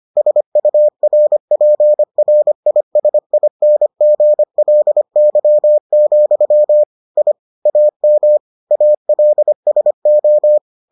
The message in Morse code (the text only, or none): SURPRISINGLY, I AM ALSO